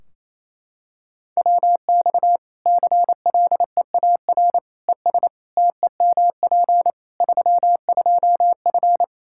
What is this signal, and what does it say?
Morse code, heard as a keyed tone, about 28 wpm, 700 Hz. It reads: WX CLEAR ES TEMP 32F
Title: Practice QSO - Exchange 2 - Info - 28wpm